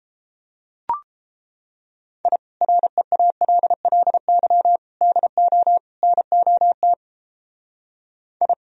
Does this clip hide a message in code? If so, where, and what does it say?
Yes: Morse code from 2.3 to 8.5 s, heard as a keyed tone, about 31 wpm, 700 Hz. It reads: I REALLY DO NOT I